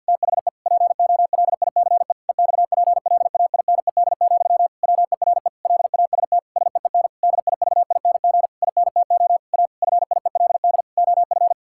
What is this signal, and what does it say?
Morse code, heard as a keyed tone, about 50 wpm, 700 Hz. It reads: THE POLICE EXPLAINED, WERE LAST SEEN DIVING INTO A FIELD OF
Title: Pangram Challenge v2 - 50wpm